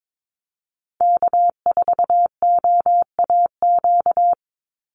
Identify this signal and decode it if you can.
Morse code, heard as a keyed tone, about 22 wpm, 690 Hz. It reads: K4OAQ
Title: Call Signs Repeated 3x - 22wpm